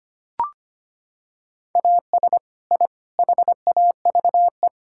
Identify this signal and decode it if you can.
Morse code, heard as a keyed tone, about 25 wpm, 700 Hz. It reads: AS I HAVE
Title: Sentences from Top 1000 Words - Review Part 6 - 25wpm